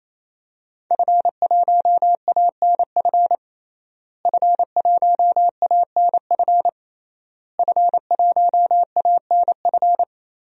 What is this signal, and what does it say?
Morse code, heard as a keyed tone, about 27 wpm, 700 Hz. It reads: F1ANF F1ANF F1ANF
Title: DXCC-100 Call Signs - 5 Characters - Repeated 3x - 28wpm